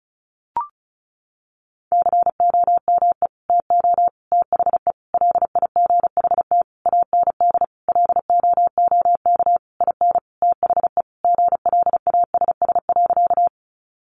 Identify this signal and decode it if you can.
Morse code, heard as a keyed tone, about 34 wpm, 700 Hz. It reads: COME TO THE LIGHT AND LOOK IN THE GLASS.
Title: Sentences from Top 1300 Words 35wpm